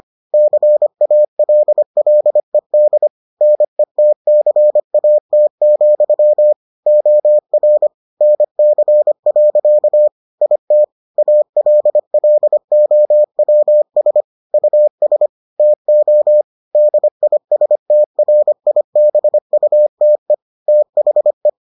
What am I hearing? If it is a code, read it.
Morse code, heard as a keyed tone, about 25 wpm, 600 Hz. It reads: CALLED NETCAT, OR NC. IT ALLOWS US TO DISTRIBUTE THE